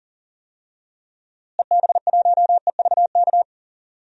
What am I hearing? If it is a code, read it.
Morse code, heard as a keyed tone, about 40 wpm, 700 Hz. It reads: EB1EVK
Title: DXCC-100 Call Signs - 6 Characters - Repeated 2x - 40wpm